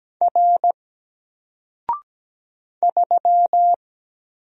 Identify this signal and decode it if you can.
Morse code, heard as a keyed tone, about 17 wpm, 700 Hz. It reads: R 3
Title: Single Character (Letter:Number:Punctuation:Prosign) 17wpm